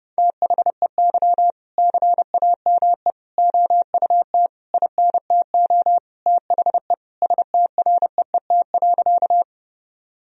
Morse code, heard as a keyed tone, about 32 wpm, 700 Hz. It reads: THEY CAME OUT INTO THE STREET.
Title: Sentences from Top 1000 Words - Review Part 3 - 30wpm